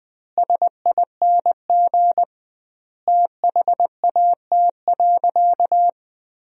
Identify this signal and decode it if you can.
Morse code, heard as a keyed tone, about 21 wpm, 700 Hz. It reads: SING THAT.
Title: 2x Word-Spacing - Sentences from Top 500 Words - 20wpm